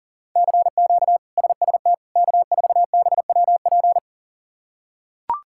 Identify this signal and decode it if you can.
Morse code, heard as a keyed tone, about 37 wpm, 700 Hz. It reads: CQ SST K4BWP
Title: K1USN SST QSO Practice - 40wpm